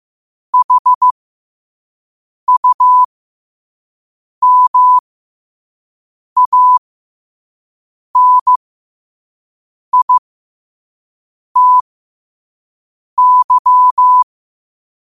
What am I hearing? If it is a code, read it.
Morse code, heard as a keyed tone, about 15 wpm, spaced as for 5 wpm, 1000 Hz. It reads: HUMANITY
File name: qotdfiles / 5wpm